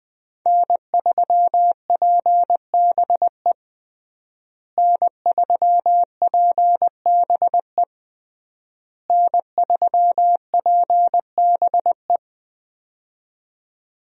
Morse code, heard as a keyed tone, about 20 wpm, 700 Hz. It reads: N3PBE N3PBE N3PBE
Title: USA Call Signs - 5 Characters - Repeated 3x - 20wpm